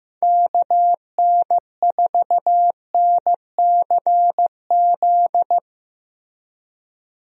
Morse code, heard as a keyed tone, about 16 wpm, 700 Hz. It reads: KN4NCZ
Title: USA Call Signs - 6 Characters - Repeated 3x - 15wpm